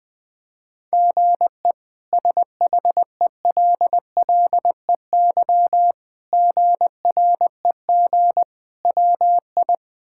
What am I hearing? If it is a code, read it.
Morse code, heard as a keyed tone, about 21 wpm, 700 Hz. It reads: GE SHELLEY GREG WI
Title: K1USN SST QSO Practice - 20wpm